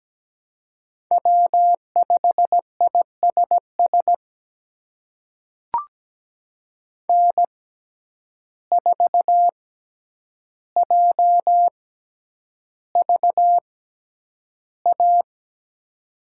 Morse code, heard as a keyed tone, about 17 wpm, spaced as for 9 wpm, 700 Hz. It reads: W5ISS N4JVA
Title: USA Call Signs - 5 Characters - 6x Character Spacing - 17wpm